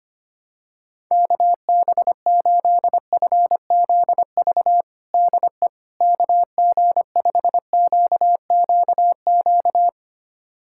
Morse code, heard as a keyed tone, about 25 wpm, 700 Hz. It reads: KB8FZV DE KG5QQQ
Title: USA Callsign DE Callsign - 6 Characters - 25wpm